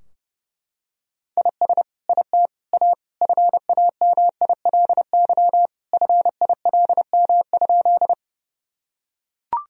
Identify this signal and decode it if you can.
Morse code, heard as a keyed tone, about 30 wpm, 700 Hz. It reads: IS IT A FAMILY FILM?
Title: Sentences from Top 1000 Words - Review Part 3 - 30wpm